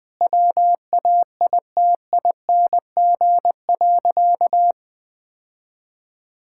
Morse code, heard as a keyed tone, about 21 wpm, 700 Hz. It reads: WAITING.
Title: Sentences with 5-Word Phrases - Spoken Beginning - 20wpm